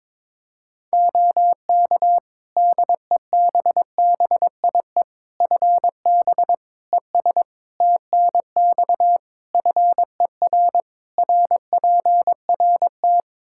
Morse code, heard as a keyed tone, about 22 wpm, 700 Hz. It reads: OK DEBBIE FB ES TNX FER RPRT
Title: Practice QSO - Exchange 2 - Info - 22wpm